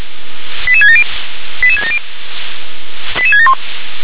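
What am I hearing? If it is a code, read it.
Five-tone selective call (ZVEI), heard as quick runs of five tones, tones of about 80 ms. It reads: 88670 88670 88671
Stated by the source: ZVEI five-tone selective call 88776 received on SDR.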